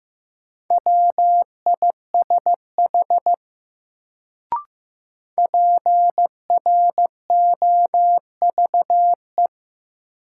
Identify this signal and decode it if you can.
Morse code, heard as a keyed tone, about 15 wpm, 700 Hz. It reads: WISH PROVE
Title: Sentences from Top 1600 Words - New Word Drill - 15wpm